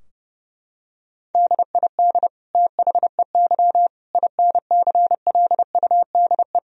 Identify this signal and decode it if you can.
Morse code, heard as a keyed tone, about 32 wpm, 700 Hz. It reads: DID THEY INCLUDE
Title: Sentences from Top 1000 Words - Review Part 3 - 30wpm